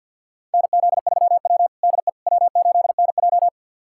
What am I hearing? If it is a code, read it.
Morse code, heard as a keyed tone, about 53 wpm, 700 Hz. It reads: NZ3W DE W8NJ
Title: USA Callsign DE Callsign - 4 Characters - 50wpm